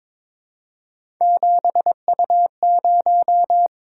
Morse code, heard as a keyed tone, about 22 wpm, 700 Hz. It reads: 7U0
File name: practice / 22wpm / Three Letter-Number Combinations - 22wpm